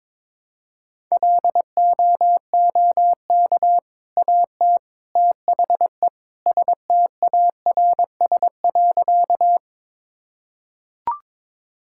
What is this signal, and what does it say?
Morse code, heard as a keyed tone, about 22 wpm, 700 Hz. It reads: LOOK AT THE STARS.
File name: practice / 22wpm / Sentences with 3-Word Phrases - 22wpm